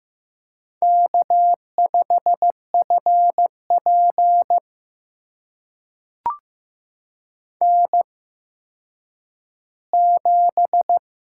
Morse code, heard as a keyed tone, about 15 wpm, spaced as for 9 wpm, 700 Hz. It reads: K5FP N7
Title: USA Call Signs - 8x Character Spacing - 15wpm